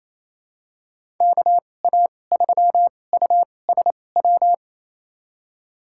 Morse code, heard as a keyed tone, about 27 wpm, spaced as for 20 wpm, 700 Hz. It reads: KA3USW
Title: USA Call Signs - 6 Characters - 2x Character Spacing - 28wpm